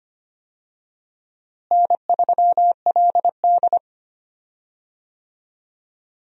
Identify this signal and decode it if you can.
Morse code, heard as a keyed tone, about 25 wpm, 700 Hz. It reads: N3LD
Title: USA Call Signs - 4 Characters - Repeated 2x - 25wpm